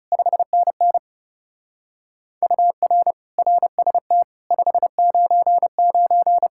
Morse code, heard as a keyed tone, about 38 wpm, 700 Hz. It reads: 5NN UR RST 599